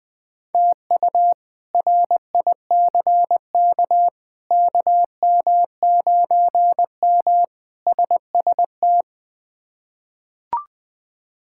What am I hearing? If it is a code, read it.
Morse code, heard as a keyed tone, about 20 wpm, 700 Hz. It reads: TU RICK KM9M SST
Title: K1USN SST QSO Practice - 20wpm